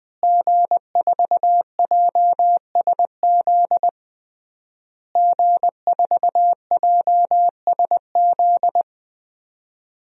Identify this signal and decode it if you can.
Morse code, heard as a keyed tone, about 20 wpm, 700 Hz. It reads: G4JSZ G4JSZ
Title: DXCC-100 Call Signs - 5 Characters - Repeated 3x - 20wpm